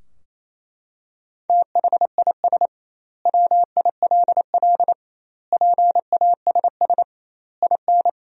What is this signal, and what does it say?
Morse code, heard as a keyed tone, about 28 wpm, 700 Hz. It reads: THIS WILL PASS IN
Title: 2x Word-Spacing - Sentences from Top 400 Words - 28wpm